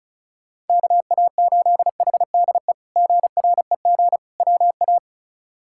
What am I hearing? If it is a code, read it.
Morse code, heard as a keyed tone, about 34 wpm, 700 Hz. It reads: KA8HDE GREG WA
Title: K1USN SST QSO Practice - 35wpm